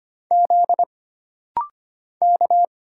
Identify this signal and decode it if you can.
Morse code, heard as a keyed tone, about 25 wpm, 700 Hz. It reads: Z K
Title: Single Letters - Rapid-Fire 25wpm